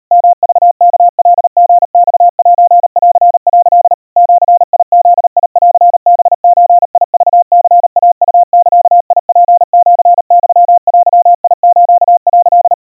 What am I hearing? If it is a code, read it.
Morse code (Japanese Wabun code), heard as a keyed tone, about 39 wpm, 700 Hz. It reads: ヨクワカリマセン」 ジブンハゾクニイウザツシユデス」
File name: kawa / Morse sample / kamyu 140 cpm 700 Hz Crispy